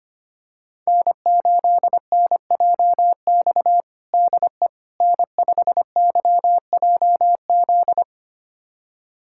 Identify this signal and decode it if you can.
Morse code, heard as a keyed tone, about 25 wpm, 700 Hz. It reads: N8NJX DE N5YJZ
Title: USA Callsign DE Callsign - 5 Characters - 25wpm